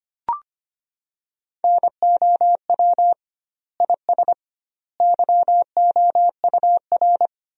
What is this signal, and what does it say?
Morse code, heard as a keyed tone, about 25 wpm, 700 Hz. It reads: NOW IS YOUR
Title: 2x Word-Spacing - Sentences from Top 100 Words - 25wpm